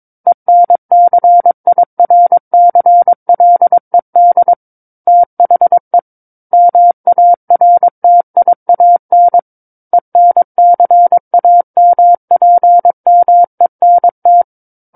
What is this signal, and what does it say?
Morse code, heard as a keyed tone, about 22 wpm, 700 Hz. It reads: ENCIRCLED THE MARTIAN ENCAMPMENT
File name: Book One / 13-Chapter 13 - 22 WPM